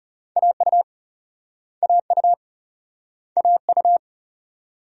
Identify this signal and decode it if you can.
Morse code, heard as a keyed tone, about 38 wpm, 700 Hz. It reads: AU AU AU